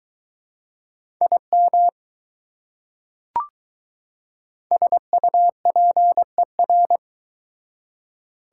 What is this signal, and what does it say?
Morse code, heard as a keyed tone, about 20 wpm, 700 Hz. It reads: IM SUPER